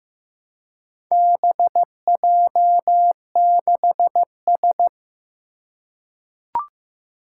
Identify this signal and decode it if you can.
Morse code, heard as a keyed tone, about 15 wpm, 700 Hz. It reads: BJ6S
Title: Four Challenging Letter-Number Combinations 15wpm